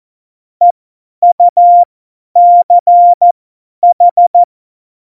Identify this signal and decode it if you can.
Morse code, heard as a keyed tone, about 14 wpm, spaced as for 10 wpm, 700 Hz. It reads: EUCH
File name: Morsekurs / Witze / Witz 23. April 2019, 50 F70